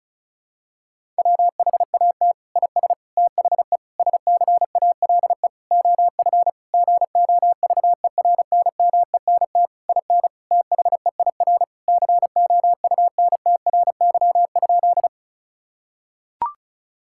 Morse code, heard as a keyed tone, about 34 wpm, 700 Hz. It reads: WHAT IS THE SCALE OF GOVERNMENT IN THEIR COUNTRY?
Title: Sentences from Top 1000 Words - Review Part 5 - 35wpm